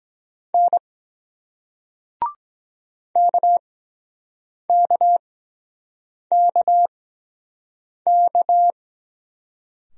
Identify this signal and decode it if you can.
Morse code, heard as a keyed tone, about 23 wpm, spaced as for 7 wpm, 700 Hz. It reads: N KKKK